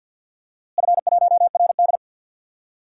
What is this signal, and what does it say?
Morse code, heard as a keyed tone, about 52 wpm, 700 Hz. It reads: F1RD
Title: DXCC-100 Call Signs - 4 Characters - Repeated 3x - 50wpm